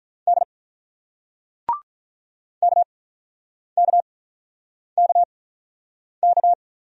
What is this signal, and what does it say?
Morse code, heard as a keyed tone, about 44 wpm, spaced as for 9 wpm, 690 Hz. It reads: D KKKK
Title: Speed-Racing Single Letters 35wpm